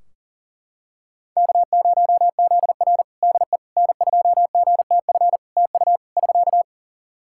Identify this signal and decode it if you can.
Morse code, heard as a keyed tone, about 37 wpm, 700 Hz. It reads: K0ZR DE N2GTF TU <SK>
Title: Practice QSO - Exchange 4 - Ending - 40wpm